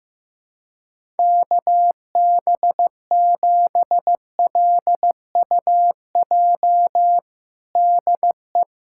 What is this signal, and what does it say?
Morse code, heard as a keyed tone, about 15 wpm, 700 Hz. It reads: KB7LUJ DE
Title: USA Callsign DE Callsign - 6 Characters - 15wpm